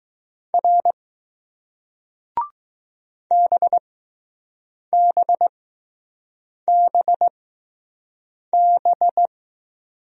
Morse code, heard as a keyed tone, about 20 wpm, spaced as for 6 wpm, 700 Hz. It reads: R BBBB